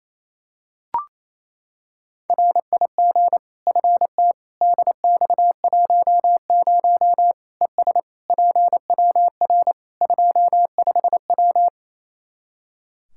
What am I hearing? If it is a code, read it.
Morse code, heard as a keyed tone, about 27 wpm, 700 Hz. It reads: RIG FT DX10 ES PWR 25W
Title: Practice QSO - Exchange 2 - Info - 28wpm